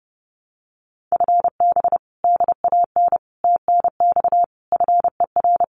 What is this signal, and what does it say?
Morse code, heard as a keyed tone, about 30 wpm, 700 Hz. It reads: FB DAN TNX FER
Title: Common QSO Phrases Encore 30wpm